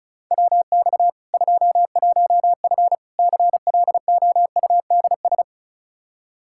Morse code, heard as a keyed tone, about 34 wpm, 700 Hz. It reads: WX 21F CLOUDS
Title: K1USN SST QSO Practice 35wpm